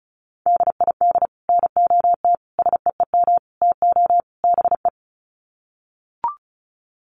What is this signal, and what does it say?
Morse code, heard as a keyed tone, about 34 wpm, 700 Hz. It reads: DID NOT SEEM TO BE
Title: Sets of 5 Words Encore 35wpm